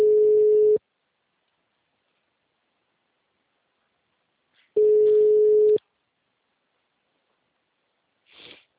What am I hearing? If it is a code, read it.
Ringback tone (Europe, 425 Hz).
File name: out- -2003- - - .724